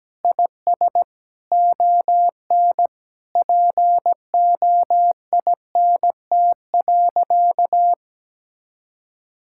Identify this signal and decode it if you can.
Morse code, heard as a keyed tone, about 17 wpm, 700 Hz. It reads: IS ON POINT.
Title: Sentences from Top 1000 Words - Review Part 5 - 17wpm